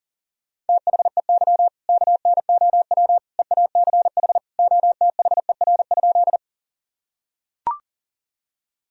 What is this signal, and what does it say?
Morse code, heard as a keyed tone, about 40 wpm, 700 Hz. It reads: THEY KNOW EACH OTHER?
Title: Sentences from Top 1000 Words - Review Part 1 - 40wpm